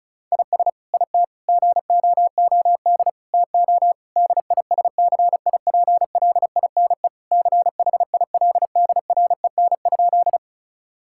Morse code, heard as a keyed tone, about 36 wpm, 700 Hz. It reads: IS IT GOOD TO DISCIPLINE CHILDREN?
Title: Sentences from Top 1000 Words - Review Part 6 - 35wpm